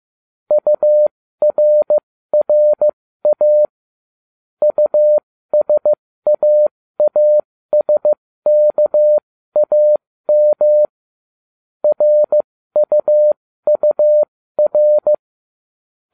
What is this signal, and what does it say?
Morse code, heard as a keyed tone, about 15 wpm, spaced as for 13 wpm, 600 Hz. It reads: URRA USAASKAM RUUR